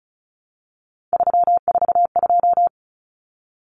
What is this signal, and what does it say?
Morse code, heard as a keyed tone, about 34 wpm, 700 Hz. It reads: 342
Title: Numbers - 3-Digit 35wpm